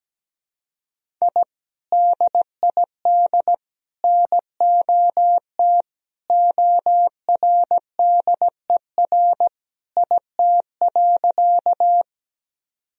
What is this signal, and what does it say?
Morse code, heard as a keyed tone, about 17 wpm, 700 Hz. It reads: I DID NOT ORDER IT.
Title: Sentences from Top 1000 Words - Review Part 2 - 17wpm